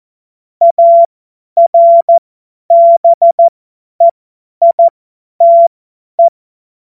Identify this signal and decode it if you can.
Morse code, heard as a keyed tone, about 14 wpm, spaced as for 10 wpm, 680 Hz. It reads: ARBEITE